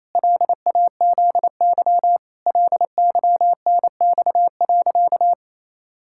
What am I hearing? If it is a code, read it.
Morse code, heard as a keyed tone, about 28 wpm, 700 Hz. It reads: LAZY LYNX.
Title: Pangram Challenge v2 - 28wpm